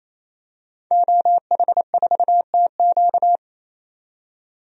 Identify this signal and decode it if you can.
Morse code, heard as a keyed tone, about 28 wpm, 700 Hz. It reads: OH4TQ
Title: DXCC-100 Call Signs - 5 Characters - 28wpm